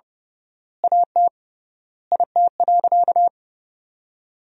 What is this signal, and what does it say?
Morse code, heard as a keyed tone, about 30 wpm, 700 Hz. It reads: AT IT.
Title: 3x Word-Spacing - Sentences from Top 300 Words - 30wpm